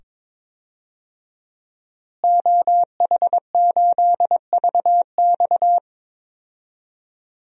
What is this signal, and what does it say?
Morse code, heard as a keyed tone, about 22 wpm, 700 Hz. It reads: OH8VX
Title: DXCC-100 Call Signs - 5 Characters - Repeated 2x - 22wpm